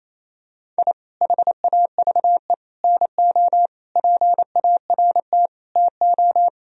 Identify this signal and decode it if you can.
Morse code, heard as a keyed tone, about 28 wpm, 700 Hz. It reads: I HAVE NO PART TO
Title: Sentences from Top 1000 Words - Review Part 1 - 28wpm